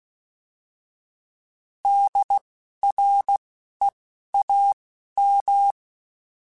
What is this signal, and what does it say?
Morse code, heard as a keyed tone, about 16 wpm, spaced as for 12 wpm, 800 Hz. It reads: DREAM
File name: Alphabetical / Alphabetical mp3s / Ds @10